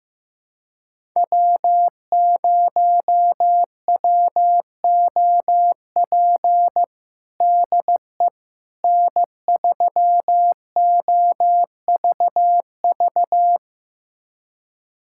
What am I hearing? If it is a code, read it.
Morse code, heard as a keyed tone, about 15 wpm, 700 Hz. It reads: W0WOP DE N3OVV
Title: USA Callsign DE Callsign - 5 Characters - 15wpm